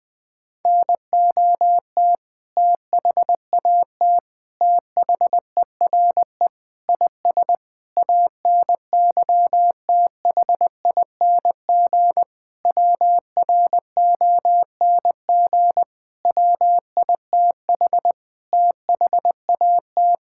Morse code, heard as a keyed tone, about 20 wpm, 700 Hz. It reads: NOT THAT THERE IS ANYTHING WRONG WITH THAT
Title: Sentences with 7-Word Phrases - 20wpm